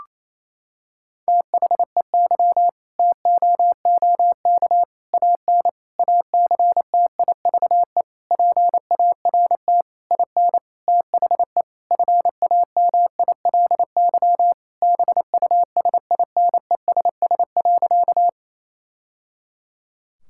Morse code, heard as a keyed tone, about 28 wpm, 700 Hz. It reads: THEY TOOK AN ACTIVE PART IN THE FAMILY BUSINESS.
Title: Sentences from Top 1000 Words - Review Part 6 - 28wpm